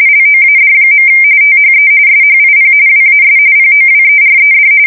RTTY, about 45.45 baud. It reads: LOAD. THE SHIP WAS POWERED
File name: fsk170 / USS-Massachusetts